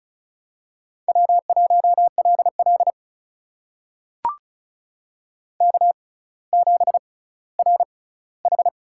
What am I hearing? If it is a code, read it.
Morse code, heard as a keyed tone, about 36 wpm, spaced as for 18 wpm, 700 Hz. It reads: W1LL K7RH
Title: USA Call Signs - 4 Characters - 6x Character Spacing - 35wpm